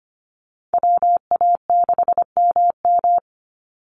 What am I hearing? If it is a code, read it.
Morse code, heard as a keyed tone, about 25 wpm, 700 Hz. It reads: WA6MM
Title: NA SOTA Chaser Callsigns 25wpm